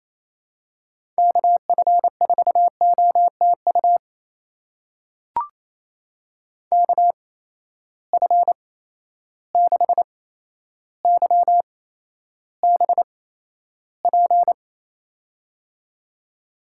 Morse code, heard as a keyed tone, about 28 wpm, spaced as for 12 wpm, 700 Hz. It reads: KF4OTU KF6YBP
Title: USA Call Signs - 6 Characters - 8x Character Spacing - 28wpm